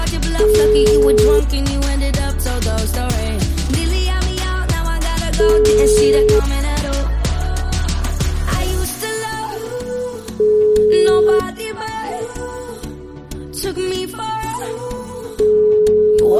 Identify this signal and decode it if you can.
Ringback tone (Europe, 425 Hz), with speech.